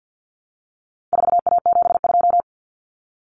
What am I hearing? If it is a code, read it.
Morse code, heard as a keyed tone, about 50 wpm, 700 Hz. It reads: 4A72
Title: Common Cut Numbers - 4-Digit 50wpm